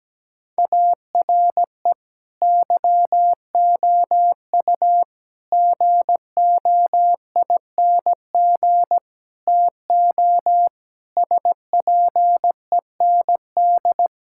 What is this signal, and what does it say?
Morse code, heard as a keyed tone, about 17 wpm, 700 Hz. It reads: ARE YOU GOING TO SPEND
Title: Sentences from Top 1000 Words - Review Part 5 - 17wpm